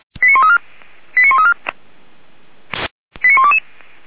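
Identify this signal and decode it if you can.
Five-tone selective call (ZVEI), heard as quick runs of five tones, tones of about 70 ms. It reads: 89134 89134 89122
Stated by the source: ZVEI five-tone call 89103 received by SDR.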